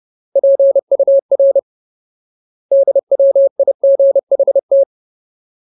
Morse code, heard as a keyed tone, about 30 wpm, 550 Hz. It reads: PUR DWIGHT
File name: Ultimate Word List / 30WPM Ultimate